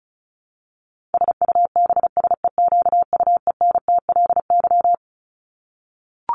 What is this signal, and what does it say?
Morse code, heard as a keyed tone, about 34 wpm, 700 Hz. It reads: SUBSEQUENTLY
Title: LY Suffix - Top 100 Words with Base Words 35wpm